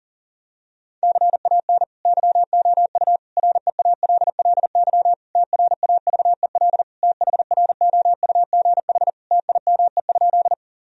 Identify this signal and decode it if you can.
Morse code, heard as a keyed tone, about 40 wpm, 700 Hz. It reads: CAN YOU REALLY TRAVEL THROUGH TIME?
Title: Sentences from Top 1000 Words - Review Part 2 - 40wpm